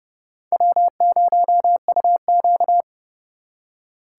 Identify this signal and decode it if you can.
Morse code, heard as a keyed tone, about 39 wpm, 700 Hz. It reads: W0UQ